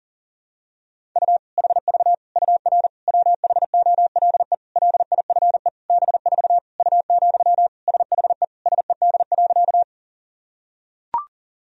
Morse code, heard as a keyed tone, about 40 wpm, 700 Hz. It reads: U HV UR WHOLE LIFE B4 U, SHE SED.
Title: Sentences from Top 400 Words and Abbreviations 40wpm